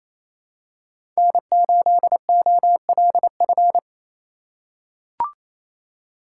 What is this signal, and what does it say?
Morse code, heard as a keyed tone, about 28 wpm, 700 Hz. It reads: N8OLF
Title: USA Call Signs - 4x Character Spacing - 28wpm